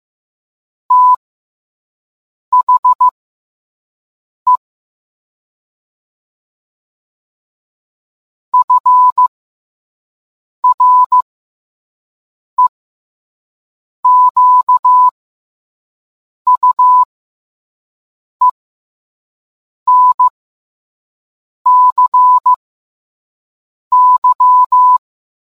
Morse code, heard as a keyed tone, about 15 wpm, spaced as for 5 wpm, 1000 Hz. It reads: THE FREQUENCY